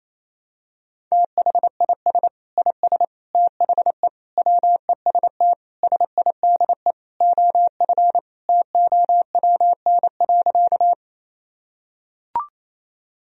Morse code, heard as a keyed tone, about 28 wpm, 700 Hz. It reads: THIS IS THE WEST SIDE OF TOWN.
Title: Sentences from Top 1000 Words - Review Part 2 - 28wpm